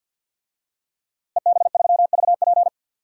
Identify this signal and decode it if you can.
Morse code, heard as a keyed tone, about 47 wpm, 700 Hz. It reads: EB3VP
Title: DXCC-100 Call Signs - 5 Characters - Repeated 3x - 50wpm